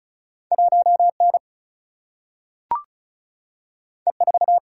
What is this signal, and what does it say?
Morse code, heard as a keyed tone, about 36 wpm, 700 Hz. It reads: 1 N E 4
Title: Single-Letter Single-Number 35wpm